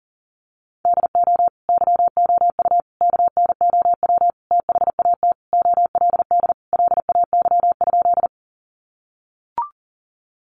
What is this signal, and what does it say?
Morse code, heard as a keyed tone, about 40 wpm, 700 Hz. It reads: DO YOU KNOW THAT OLD LAY?
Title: Sentences from Top 400 Words 40wpm